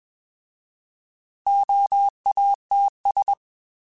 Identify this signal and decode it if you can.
Morse code, heard as a keyed tone, about 21 wpm, 800 Hz. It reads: OATS
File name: Alphabetical / Alphabetical mp3s / Os @20